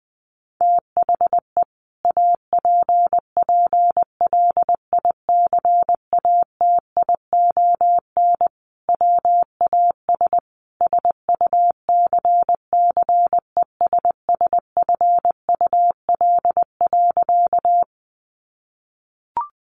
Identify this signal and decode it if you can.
Morse code, heard as a keyed tone, about 20 wpm, 700 Hz. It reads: THE APPLICATION WAS SUCCESSFUL.
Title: Sentences from Top 1200 Words 20wpm